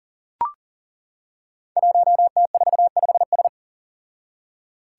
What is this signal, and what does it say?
Morse code, heard as a keyed tone, about 40 wpm, 700 Hz. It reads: 1T45S
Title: Five Challenging Letter-Number Combinations 40wpm